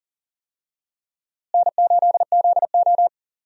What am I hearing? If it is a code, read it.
Morse code, heard as a keyed tone, about 39 wpm, 700 Hz. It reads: N8ZO
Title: USA Call Signs - 4 Characters - Repeated 2x - 40wpm